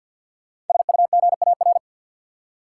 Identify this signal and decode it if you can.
Morse code, heard as a keyed tone, about 53 wpm, 700 Hz. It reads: SUGAR
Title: Five Letter Words - 50wpm